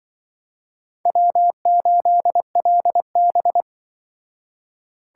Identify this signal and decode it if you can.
Morse code, heard as a keyed tone, about 24 wpm, 700 Hz. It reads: W8LB
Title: Speed Racing - USA 1x2 Call Signs 20wpm